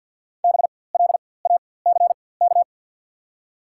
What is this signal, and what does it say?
Morse code, heard as a keyed tone, about 50 wpm, spaced as for 23 wpm, 700 Hz. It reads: BLACK
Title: 4x Character Spacing - Top 500 Words - 50wpm